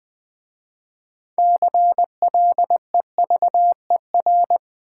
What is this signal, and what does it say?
Morse code, heard as a keyed tone, about 20 wpm, 700 Hz. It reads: CLEVER
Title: 6 Letter Words - First 3 Letters Given - 20wpm